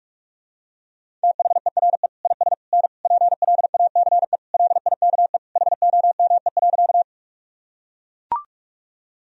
Morse code, heard as a keyed tone, about 42 wpm, 700 Hz. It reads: THERE IS N PLACE LIKE HOME.
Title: Sentences from Top 200 Words and Abbreviations 45wpm